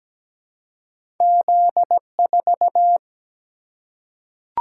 Morse code, heard as a keyed tone, about 17 wpm, 700 Hz. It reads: Z4
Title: Single-Letter Single-Number 17wpm